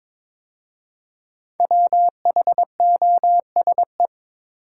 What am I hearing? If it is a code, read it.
Morse code, heard as a keyed tone, about 22 wpm, 700 Hz. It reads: WHOSE
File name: practice / 22wpm / 5 Letter Words - First 3 Letters Given - 22wpm